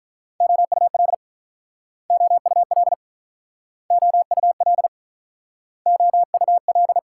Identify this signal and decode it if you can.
Morse code, heard as a keyed tone, about 47 wpm, 700 Hz. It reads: OUL OUL OUL OUL